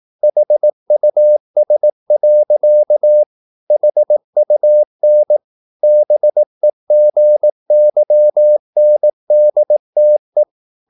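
Morse code, heard as a keyed tone, about 18 wpm, 600 Hz. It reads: HUS. HUN BEGYNDTE
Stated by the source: Slow (QRS) Morse practice at 18 wpm, CW med Gnister.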